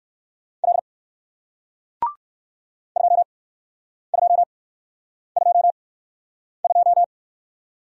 Morse code, heard as a keyed tone, about 56 wpm, spaced as for 9 wpm, 700 Hz. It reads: L 2222
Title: Speed-Racing Single Letter-Number 45wpm